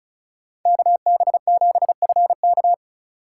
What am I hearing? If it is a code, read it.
Morse code, heard as a keyed tone, about 36 wpm, 700 Hz. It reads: KB7FK
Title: USA Call Signs - 6x Character Spacing - 35wpm